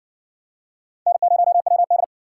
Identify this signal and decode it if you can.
Morse code, heard as a keyed tone, about 58 wpm, 700 Hz. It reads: N0WD